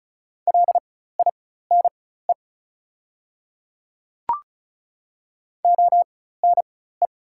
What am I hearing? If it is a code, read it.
Morse code, heard as a keyed tone, about 36 wpm, spaced as for 16 wpm, 700 Hz. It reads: LINE ONE
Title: 4x Character Spacing - Top 200 Words - 35wpm